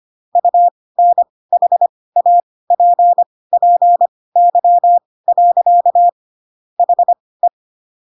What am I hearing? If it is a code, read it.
Morse code, heard as a keyed tone, about 25 wpm, spaced as for 18 wpm, 700 Hz. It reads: UNHAPPY. HE